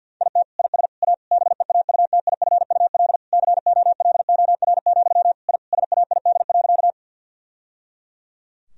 Morse code, heard as a keyed tone, about 52 wpm, 700 Hz. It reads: IT IS A BEAUTIFUL COLOR, I SAID.
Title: Sentences from Top 1000 Words - Review Part 2 - 50wpm